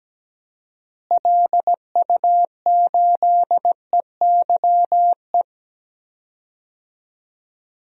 Morse code, heard as a keyed tone, about 17 wpm, 700 Hz. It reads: LU8EYE
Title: DXCC-100 Call Signs - 6 Characters - Repeated 2x - 17wpm